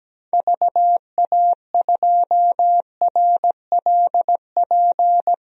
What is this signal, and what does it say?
Morse code, heard as a keyed tone, about 17 wpm, 700 Hz. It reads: VA2RLP
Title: DXCC-100 Call Signs - 6 Characters - 17wpm